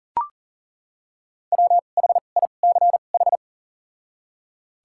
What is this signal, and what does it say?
Morse code, heard as a keyed tone, about 37 wpm, spaced as for 28 wpm, 700 Hz. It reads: WHICH
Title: 2x Character Spacing - Top 300 Words - 40wpm